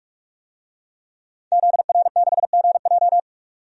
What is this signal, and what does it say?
Morse code, heard as a keyed tone, about 41 wpm, 700 Hz. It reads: ZR6GJ
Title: DXCC-100 Call Signs - 5 Characters - 45wpm